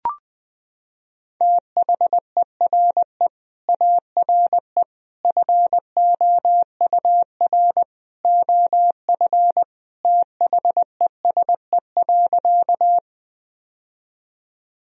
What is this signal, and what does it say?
Morse code, heard as a keyed tone, about 20 wpm, 700 Hz. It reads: THERE ARE FOUR OF THESE.
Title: Sentences from Top 1000 Words - Review Part 1 - 20wpm